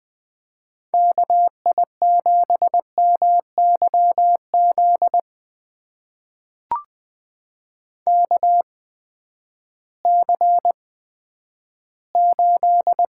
Morse code, heard as a keyed tone, about 20 wpm, spaced as for 11 wpm, 700 Hz. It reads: KI7MYZ KC8
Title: USA Call Signs - 6 Characters - 8x Character Spacing - 20wpm